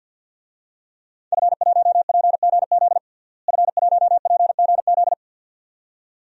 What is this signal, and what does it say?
Morse code, heard as a keyed tone, about 51 wpm, 700 Hz. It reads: F1PGZ F1PGZ
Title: DXCC-100 Call Signs - 5 Characters - Repeated 2x - 50wpm